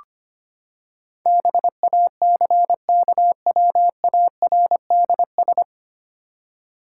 Morse code, heard as a keyed tone, about 25 wpm, 700 Hz. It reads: BACKWARDS
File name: Nine Letter Words - 25wpm